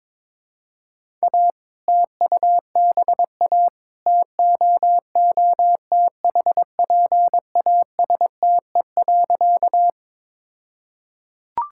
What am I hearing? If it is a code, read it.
Morse code, heard as a keyed tone, about 22 wpm, 700 Hz. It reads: A TUBA TOOTHPASTE.
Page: morse-code-ninja-practice-22wpm